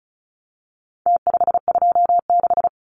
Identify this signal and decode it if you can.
Morse code, heard as a keyed tone, about 34 wpm, 700 Hz. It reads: T526
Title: Numbers - Contest Style - 4-Digit 35wpm